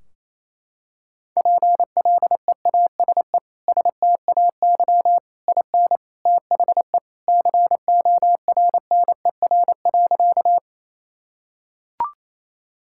Morse code, heard as a keyed tone, about 27 wpm, 700 Hz. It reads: PLEASE STAY IN THE CORNER.
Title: Sentences from Top 1600 Words 28wpm